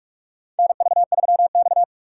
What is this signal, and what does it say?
Morse code, heard as a keyed tone, about 42 wpm, 700 Hz. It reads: NV3X
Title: USA Call Signs - 4 Characters - 4x Character Spacing - 45wpm